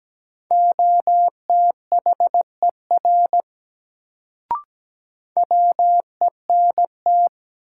Morse code, heard as a keyed tone, about 17 wpm, 700 Hz. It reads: OTHER WENT
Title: Sets of 2 Words - New Word Drill - 17wpm